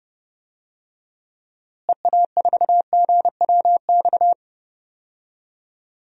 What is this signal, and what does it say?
Morse code, heard as a keyed tone, about 30 wpm, 700 Hz. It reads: EA4GWX
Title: DXCC-100 Call Signs - 6 Characters - Repeated 2x - 30wpm